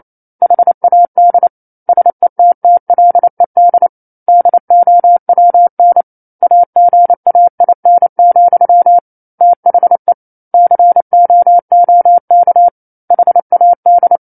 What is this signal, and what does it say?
Morse code, heard as a keyed tone, about 28 wpm, 700 Hz. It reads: HAD SETTLED DOWN AGAIN, THE COOK HAD
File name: 09-Chapter-09 28 WPM